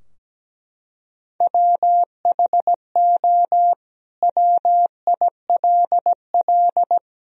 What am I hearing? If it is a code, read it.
Morse code, heard as a keyed tone, about 17 wpm, 700 Hz. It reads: WHO WILL
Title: Sentences from Top 1000 Words - Review Part 2 - 17wpm